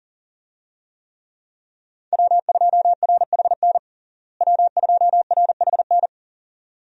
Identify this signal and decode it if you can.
Morse code, heard as a keyed tone, about 40 wpm, 700 Hz. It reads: W2RHN W2RHN
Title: USA Call Signs - 5 Characters - Repeated 2x - 40wpm